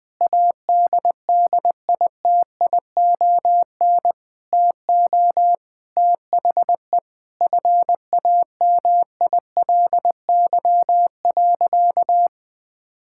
Morse code, heard as a keyed tone, about 21 wpm, 700 Hz. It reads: ADDITION TO THE FAMILY.
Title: Sentences from Top 1000 Words - Review Part 6 - 20wpm